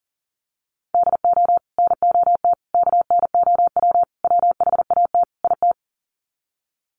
Morse code, heard as a keyed tone, about 40 wpm, 700 Hz. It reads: DO NOT KNOW WHAT IT
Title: Sets of 5 Words 40wpm